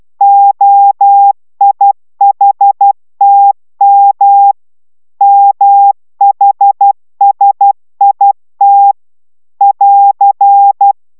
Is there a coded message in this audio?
Yes: Morse code from 0.2 to 10.9 s, heard as a keyed tone, about 12 wpm, 800 Hz. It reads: OIHTM MHSIT <AR>